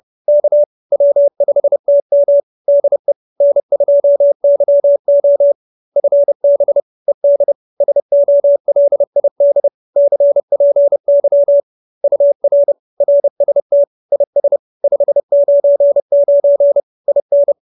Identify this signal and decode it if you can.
Morse code, heard as a keyed tone, about 30 wpm, 570 Hz. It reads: K W5TM DE N2YO FB ED SOLID CPY UR RST IS 599 IN
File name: QSO316 30